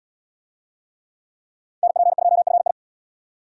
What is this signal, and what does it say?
Morse code, heard as a keyed tone, about 74 wpm, 700 Hz. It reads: NP2FI